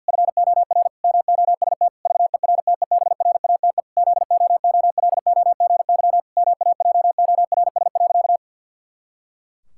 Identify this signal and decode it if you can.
Morse code, heard as a keyed tone, about 52 wpm, 700 Hz. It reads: FOR MOST VERTEBRATE ZOOLOGY MAJORS.
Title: Pangram Challenge v2 - 50wpm